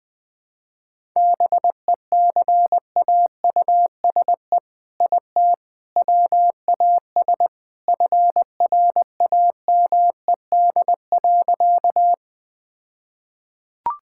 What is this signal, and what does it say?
Morse code, heard as a keyed tone, about 21 wpm, 700 Hz. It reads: BECAUSE IT WAS FRAMED.
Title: All-Age Appropriate Jokes - 20wpm